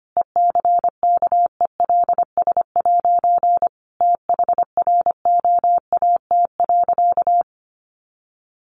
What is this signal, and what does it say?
Morse code, heard as a keyed tone, about 25 wpm, 700 Hz. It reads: ECKELS' THROAT.
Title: A Sound Of Thunder 25wpm